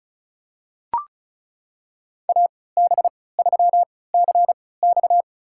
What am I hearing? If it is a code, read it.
Morse code, heard as a keyed tone, about 36 wpm, spaced as for 20 wpm, 700 Hz. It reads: AB3CX
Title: USA Call Signs - 3x Character Spacing - 35wpm